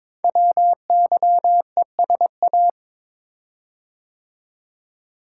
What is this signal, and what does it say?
Morse code, heard as a keyed tone, about 23 wpm, 700 Hz. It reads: WYESA
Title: Code Groups - 5 Letters - 22wpm